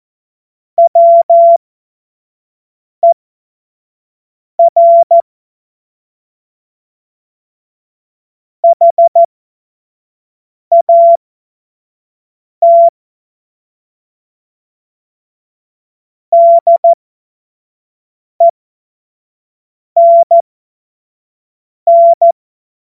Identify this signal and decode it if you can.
Morse code, heard as a keyed tone, about 14 wpm, spaced as for 5 wpm, 680 Hz. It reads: WER HAT DENN
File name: Morsekurs / Witze / Witz 22. Juni 2021, 25 F70